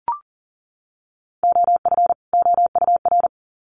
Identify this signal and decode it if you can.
Morse code, heard as a keyed tone, about 40 wpm, 700 Hz. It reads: OF OUR
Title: Sets of 2 Words 40wpm